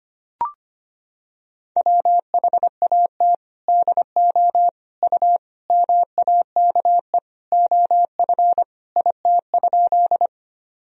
Morse code, heard as a keyed tone, about 25 wpm, 700 Hz. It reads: WHAT DO U MAKE OF IT?
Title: Sentences from Top 100 Words and Abbreviations 25wpm